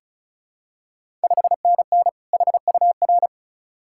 Morse code, heard as a keyed tone, about 36 wpm, 700 Hz. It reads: 5NN HUR
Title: Ontario QSO Party Exchange 35wpm